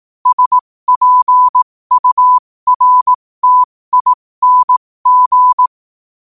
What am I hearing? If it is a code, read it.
Morse code, heard as a keyed tone, about 18 wpm, spaced as for 15 wpm, 1000 Hz. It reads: SPURTING
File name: 15wpm